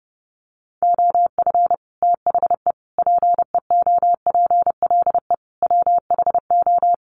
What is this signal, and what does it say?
Morse code, heard as a keyed tone, about 30 wpm, 700 Hz. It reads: OF THE PEOPLE WHO
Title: Sets of 4 Words 30wpm